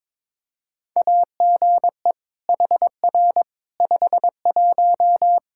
Morse code, heard as a keyed tone, about 22 wpm, 700 Hz. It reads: AGE HR 51
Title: Practice QSO - Exchange 3 - More Info - 22wpm